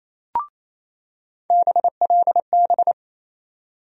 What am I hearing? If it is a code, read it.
Morse code, heard as a keyed tone, about 27 wpm, 700 Hz. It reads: BLB
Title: USA Call Sign Postfixes - 3 Letters 28wpm